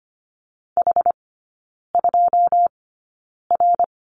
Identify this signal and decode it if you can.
Morse code, heard as a keyed tone, about 25 wpm, spaced as for 9 wpm, 700 Hz. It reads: H2R
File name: course / 25:10wpm / 093 Lesson 16e - Letter c - Three Characters 25wpm